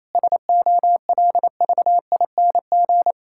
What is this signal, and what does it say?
Morse code, heard as a keyed tone, about 28 wpm, 700 Hz. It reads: SOLVING
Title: Pangram Challenge v2 - 28wpm